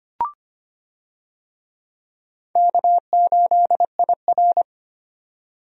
Morse code, heard as a keyed tone, about 25 wpm, 700 Hz. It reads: K8IR
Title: USA Call Signs - 4 Characters - 25wpm